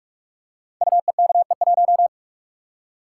Morse code, heard as a keyed tone, about 44 wpm, 700 Hz. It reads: UEKE1
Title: Code Groups - 5 Letter:Numbers - 45wpm